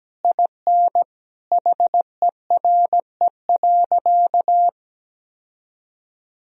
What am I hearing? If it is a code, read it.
Morse code, heard as a keyed tone, about 17 wpm, 700 Hz. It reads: IN HERE.
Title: Sentences from Top 1000 Words - Review Part 2 - 17wpm